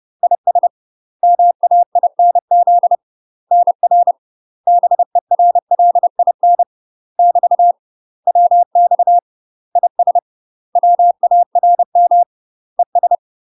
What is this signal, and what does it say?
Morse code, heard as a keyed tone, about 30 wpm, 700 Hz. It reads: IS MAINZ NR BERLIN <BT> WX IS WARM ES
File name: UK MASTER QSOs TRAINING/30WPM